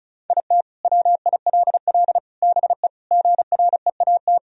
Morse code, heard as a keyed tone, about 36 wpm, 700 Hz. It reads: IT WILL BE GREAT
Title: Sentences from Top 1000 Words - Review Part 4 - 35wpm